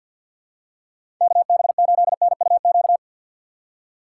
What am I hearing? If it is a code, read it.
Morse code, heard as a keyed tone, about 52 wpm, 700 Hz. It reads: KB8NUQ
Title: USA Call Signs - 6 Characters - 4x Character Spacing - 50wpm